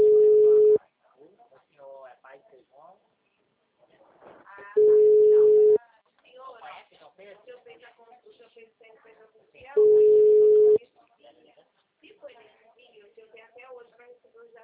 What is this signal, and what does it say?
Ringback tone (Europe, 425 Hz), with speech.
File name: out- -2009- - - .14014